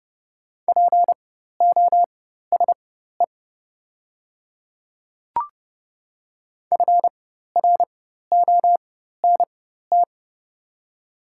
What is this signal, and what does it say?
Morse code, heard as a keyed tone, about 30 wpm, spaced as for 14 wpm, 690 Hz. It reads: POSE FRONT
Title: 4x Character Spacing - Top 500 Words - 30wpm